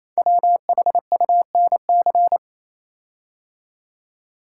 Morse code, heard as a keyed tone, about 27 wpm, 700 Hz. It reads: WHUNC
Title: Code Groups - 5 Letters - 28wpm